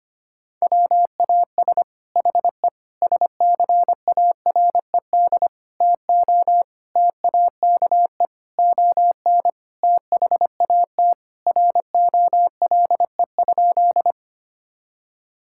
Morse code, heard as a keyed tone, about 25 wpm, 700 Hz. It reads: WAS HE SCARED TO TAKE ON THAT ROLE?
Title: Sentences from Top 1000 Words - Review Part 6 - 25wpm